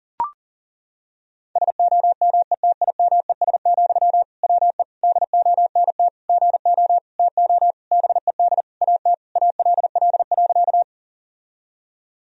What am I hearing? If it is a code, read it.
Morse code, heard as a keyed tone, about 40 wpm, 700 Hz. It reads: SOMETIMES, WE DONT GO TO BED AT ALL.
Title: Sentences from Top 1000 Words - Review Part 3 - 40wpm